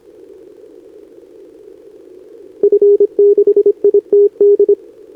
Morse code, heard as a keyed tone, about 26 wpm, 410 Hz. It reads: F6ITD
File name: CQWWCW2025 / NedeleOdpol 20m Scan